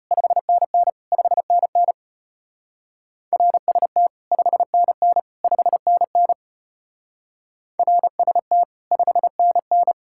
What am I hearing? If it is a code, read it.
Morse code, heard as a keyed tone, about 31 wpm, 700 Hz. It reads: 5NN 5NN RST 5NN 5NN RST 5NN
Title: Speed-Racing Common QSO Phrases - 25wpm